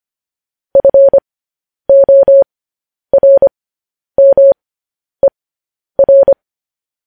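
Morse code, heard as a keyed tone, about 25 wpm, spaced as for 10 wpm, 550 Hz. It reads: FORMER
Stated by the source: Morse code news headlines at 10 wpm.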